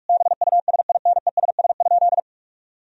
Morse code, heard as a keyed tone, about 46 wpm, 700 Hz. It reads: BUSINESS?
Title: Sentences from Top 1000 Words - Review Part 4 - 45wpm